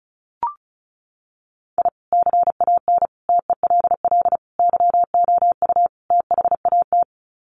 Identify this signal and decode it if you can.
Morse code, heard as a keyed tone, about 34 wpm, 700 Hz. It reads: I CAN TELL YOU THAT
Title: Sets of 5 Words 35wpm